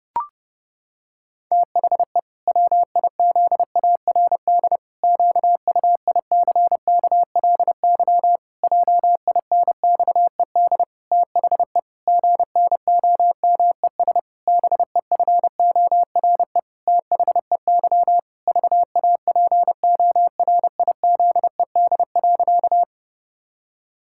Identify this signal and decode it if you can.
Morse code, heard as a keyed tone, about 30 wpm, 700 Hz. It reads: THE WIZARD QUICKLY JINXED THE GNOMES BEFORE THEY VAPORIZED.
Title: Pangram Challenge v2 - 30wpm